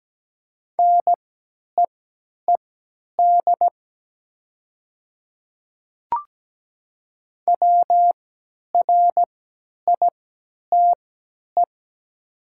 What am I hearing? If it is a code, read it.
Morse code, heard as a keyed tone, about 17 wpm, spaced as for 10 wpm, 700 Hz. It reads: NEED WRITE
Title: 3x Character Spacing - Top 200 Words - 17wpm